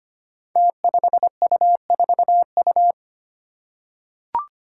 Morse code, heard as a keyed tone, about 25 wpm, 700 Hz. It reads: T5U4U
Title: Five Challenging Letter-Number Combinations 25wpm